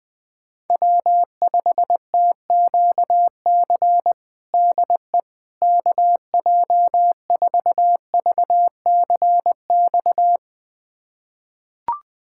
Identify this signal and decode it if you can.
Morse code, heard as a keyed tone, about 21 wpm, 700 Hz. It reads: W5TQC DE KJ4VCX
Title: USA Callsign DE Callsign 20wpm